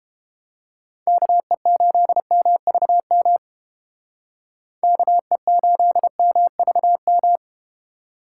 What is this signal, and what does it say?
Morse code, heard as a keyed tone, about 32 wpm, 700 Hz. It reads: KE8MVM KE8MVM